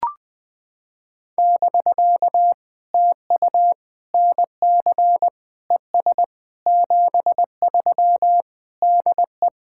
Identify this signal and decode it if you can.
Morse code, heard as a keyed tone, about 19 wpm, 700 Hz. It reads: <BK> TU NC ES 73 DE
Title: POTA QSO Practice - 20wpm